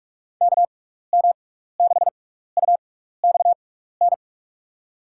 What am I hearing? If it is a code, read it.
Morse code, heard as a keyed tone, about 46 wpm, spaced as for 15 wpm, 700 Hz. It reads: KM6UXN
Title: USA Call Signs - 6x Character Spacing - 45wpm